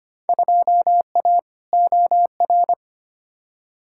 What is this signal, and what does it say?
Morse code, heard as a keyed tone, about 25 wpm, 700 Hz. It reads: 2A OR
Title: Field Day Exchanges - 25wpm